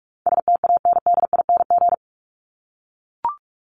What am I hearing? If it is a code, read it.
Morse code, heard as a keyed tone, about 44 wpm, 700 Hz. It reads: STANDING
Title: ING Suffix - Top 100 Words with Base Words 45wpm